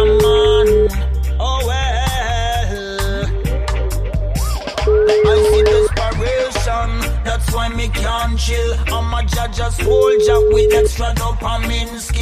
Ringback tone (Europe, 425 Hz), with noise.